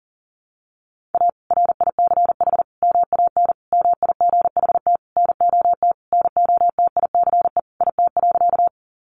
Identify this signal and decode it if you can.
Morse code, heard as a keyed tone, about 37 wpm, 700 Hz. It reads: A RICH MAN MIGHT NOT NOTICE IT.
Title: Sentences from Top 1100 Words 40wpm